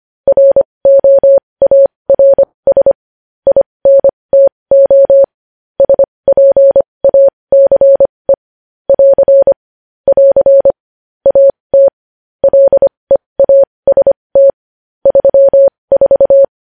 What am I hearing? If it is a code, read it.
Morse code, heard as a keyed tone, about 25 wpm, spaced as for 20 wpm, 550 Hz. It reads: ROARS INTO SPACE <AR> <AR> AT LEAST 34
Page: news-headlines-in-morse-code-at-20-wpm